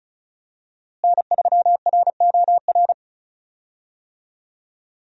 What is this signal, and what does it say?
Morse code, heard as a keyed tone, about 34 wpm, 700 Hz. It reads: N3ROR
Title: USA Call Signs - 5 Characters - Repeated 2x - 35wpm